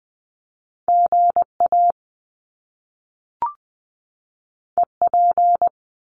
Morse code, heard as a keyed tone, about 20 wpm, 700 Hz. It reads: G A E P
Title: Top Two Letter Combinations in Words 20wpm